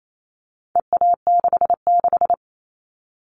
Morse code, heard as a keyed tone, about 28 wpm, 700 Hz. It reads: EA66
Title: All Cut Numbers - 4-Digit 28wpm